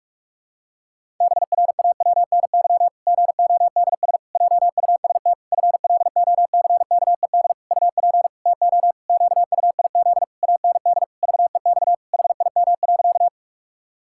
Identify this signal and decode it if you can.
Morse code, heard as a keyed tone, about 46 wpm, 700 Hz. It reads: BRAWNY GODS JUST FLOCKED UP TO QUIZ AND VEX HIM.
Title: Pangram Challenge v2 - 45wpm